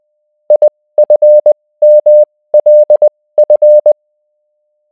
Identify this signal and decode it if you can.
Morse code, heard as a keyed tone, about 20 wpm, spaced as for 16 wpm, 600 Hz. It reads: IFMLF